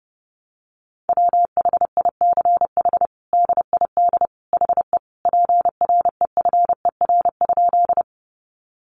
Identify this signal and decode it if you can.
Morse code, heard as a keyed tone, about 30 wpm, 700 Hz. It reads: WHICH DID HE PREFER?
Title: Sentences from Top 1100 Words 30wpm